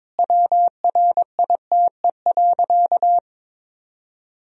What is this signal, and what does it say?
Morse code, heard as a keyed tone, about 22 wpm, 700 Hz. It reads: WRITE.
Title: Sentences from Top 1000 Words - Review Part 1 - 22wpm